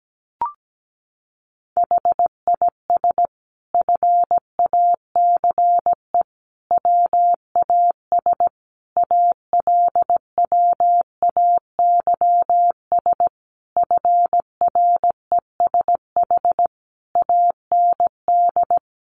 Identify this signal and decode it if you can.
Morse code, heard as a keyed tone, about 17 wpm, 700 Hz. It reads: HIS FACE WAS ALWAYS FRESH AND
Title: Sentences from Top 1200 Words 17wpm